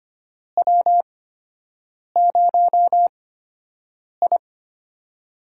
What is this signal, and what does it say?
Morse code, heard as a keyed tone, about 25 wpm, spaced as for 7 wpm, 700 Hz. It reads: W0I
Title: USA Call Signs - 8x Character Spacing - 25wpm